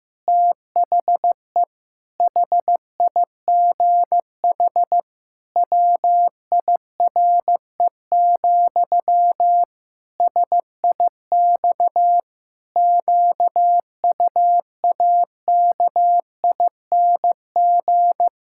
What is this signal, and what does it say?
Morse code, heard as a keyed tone, about 15 wpm, 700 Hz. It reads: THE HIGH WIRE, SIX QUAKING
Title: Pangram Challenge v2 - 15wpm